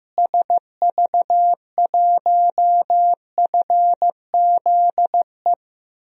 Morse code, heard as a keyed tone, about 16 wpm, 700 Hz. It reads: SV1FZE
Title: DXCC-100 Call Signs - 6 Characters - Repeated 3x - 15wpm